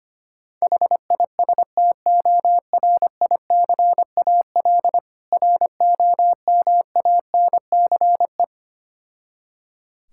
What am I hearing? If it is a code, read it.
Morse code, heard as a keyed tone, about 25 wpm, 700 Hz. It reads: HISTORICAL ROMANCE
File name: practice / 25wpm / Difficult Words with Hints - 25wpm